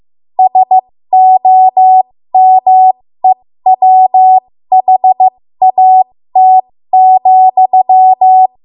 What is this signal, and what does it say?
Morse code, heard as a keyed tone, about 15 wpm, spaced as for 13 wpm, 750 Hz. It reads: SOMEWHAT,
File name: Morse / 13WPM